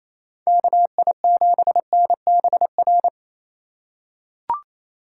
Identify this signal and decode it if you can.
Morse code, heard as a keyed tone, about 28 wpm, 700 Hz. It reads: KI7NBR
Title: USA Call Signs - 6 Characters - 2x Character Spacing - 28wpm